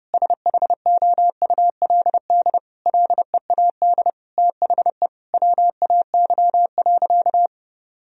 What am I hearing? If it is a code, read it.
Morse code, heard as a keyed tone, about 32 wpm, 700 Hz. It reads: SHOULD LEAD THE WAY.
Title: Sentences from Top 1000 Words - Review Part 2 - 30wpm